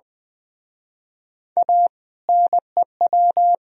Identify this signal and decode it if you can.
Morse code, heard as a keyed tone, about 21 wpm, 700 Hz. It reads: A NEW